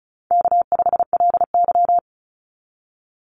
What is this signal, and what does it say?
Morse code, heard as a keyed tone, about 34 wpm, 700 Hz. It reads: K5LY
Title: Easy Call Signs Repeated 2x - 35wpm